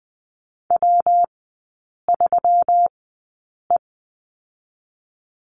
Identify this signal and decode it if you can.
Morse code, heard as a keyed tone, about 20 wpm, spaced as for 8 wpm, 700 Hz. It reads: W3E
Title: 121 Lesson 20e - Number 3 - Three Characters 20wpm